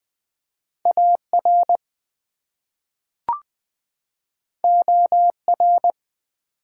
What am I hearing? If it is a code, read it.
Morse code, heard as a keyed tone, about 20 wpm, 700 Hz. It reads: AR OR
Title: US States and Canadian Provinces 20wpm